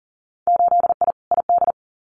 Morse code, heard as a keyed tone, about 41 wpm, 700 Hz. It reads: 8I ID
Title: Winter Field Day Exchanges Encore 40wpm